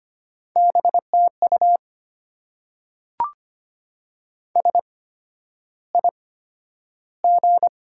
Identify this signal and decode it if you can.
Morse code, heard as a keyed tone, about 25 wpm, spaced as for 11 wpm, 700 Hz. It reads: BTU SIG
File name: Top 90 QSO Elements - 8x Character Spacing - 25wpm